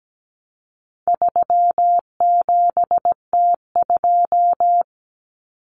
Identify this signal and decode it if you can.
Morse code, heard as a keyed tone, about 17 wpm, 700 Hz. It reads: 37T2
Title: Common Cut Numbers - 4-Digit 17wpm